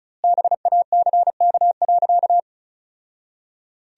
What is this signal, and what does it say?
Morse code, heard as a keyed tone, about 34 wpm, 700 Hz. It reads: BACK.
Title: 2x Word-Spacing - Sentences from Top 300 Words - 35wpm